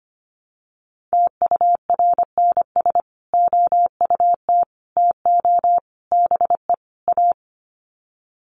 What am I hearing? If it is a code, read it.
Morse code, heard as a keyed tone, about 25 wpm, 700 Hz. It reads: TURNS OUT TO BE A
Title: Sets of 5 Words Encore 25wpm